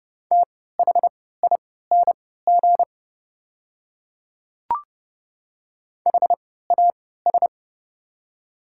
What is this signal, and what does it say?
Morse code, heard as a keyed tone, about 30 wpm, spaced as for 17 wpm, 700 Hz. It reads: THING HAS
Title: 3x Character Spacing - Top 100 Words - 30wpm